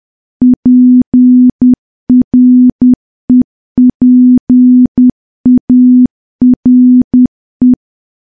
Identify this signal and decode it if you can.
Morse code, heard as a keyed tone, about 10 wpm, 250 Hz. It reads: PREPARE
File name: morse / morse-training-10WPM